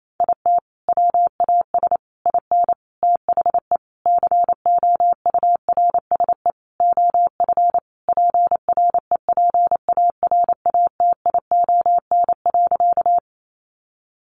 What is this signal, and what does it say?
Morse code, heard as a keyed tone, about 28 wpm, 700 Hz. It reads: IT WAS IN THE COURSE OF PREPARATION.
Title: Sentences from Top 1400 Words 28wpm